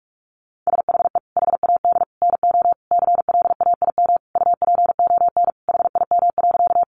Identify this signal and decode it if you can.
Morse code, heard as a keyed tone, about 44 wpm, 700 Hz. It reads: SHE HAD NO CLAIM UPON HIM.
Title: Sentences from Top 1500 Words 45wpm